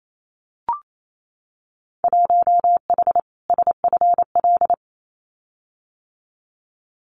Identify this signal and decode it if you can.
Morse code, heard as a keyed tone, about 28 wpm, 700 Hz. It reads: 1H SFL
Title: Winter Field Day Exchanges 28wpm